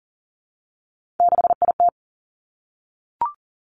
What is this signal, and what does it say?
Morse code, heard as a keyed tone, about 41 wpm, 700 Hz. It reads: 6IT
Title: Three Challenging Letter-Number Combinations 40wpm